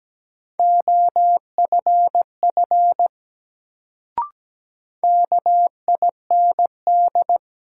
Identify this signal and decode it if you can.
Morse code, heard as a keyed tone, about 17 wpm, 700 Hz. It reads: OFF KIND
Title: Sentences from Top 200 Words - New Word Drill - 17wpm